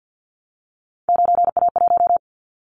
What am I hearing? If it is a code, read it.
Morse code, heard as a keyed tone, about 50 wpm, 700 Hz. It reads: 9A1
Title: Callsign Prefixes for Top 50 DXCC - 50wpm